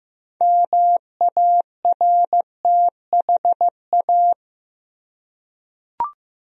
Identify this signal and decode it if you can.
Morse code, heard as a keyed tone, about 15 wpm, 700 Hz. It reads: MARTHA
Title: USA Common Names - 15wpm